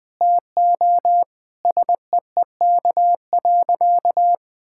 Morse code, heard as a keyed tone, about 20 wpm, 700 Hz. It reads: TO SEEK.
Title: Sentences from Top 1600 Words 20wpm